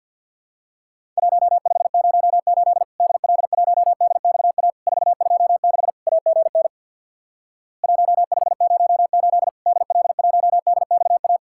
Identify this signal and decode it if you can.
Morse code, heard as a keyed tone, about 50 wpm, 700 Hz. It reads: 1508 DL1DXA 426 AGN 1508 DL1DXA
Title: Worked All Europe DX Contest - QTC Exchange with Repeats - 50wpm